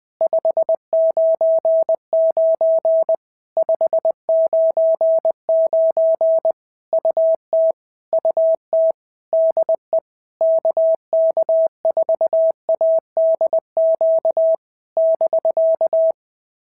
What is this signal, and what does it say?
Morse code, heard as a keyed tone, about 20 wpm, 650 Hz. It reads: 599 599 UT UT DE KK4ADQ <BK>
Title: POTA QSO Practice - 20wpm